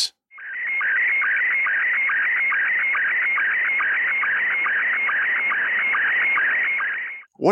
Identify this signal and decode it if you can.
SSTV (slow-scan television), Scottie 1, with speech.
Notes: an audio tour of the 10 meter band!